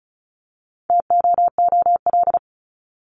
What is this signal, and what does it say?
Morse code, heard as a keyed tone, about 34 wpm, 700 Hz. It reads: TOOL
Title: Top 751-1000 Words 35wpm